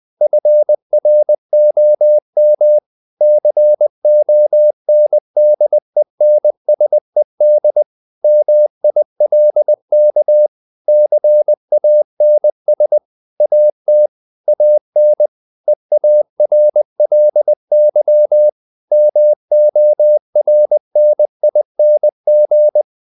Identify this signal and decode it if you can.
Morse code, heard as a keyed tone, about 20 wpm, 600 Hz. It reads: FROM CONDENSED MILK CANS AT AN EARLY MORNING
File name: the old man and the sea / the old man and the sea part 1 20wpm